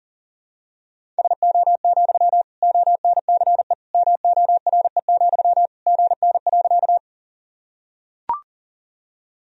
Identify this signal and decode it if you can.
Morse code, heard as a keyed tone, about 40 wpm, 700 Hz. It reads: SO, ONCE MORE, GN.
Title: Sentences from Top 300 Words and Abbreviations 40wpm